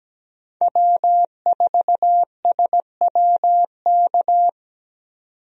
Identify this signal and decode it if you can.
Morse code, heard as a keyed tone, about 17 wpm, 700 Hz. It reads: W4SWK
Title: USA Call Signs - 5 Characters - 17wpm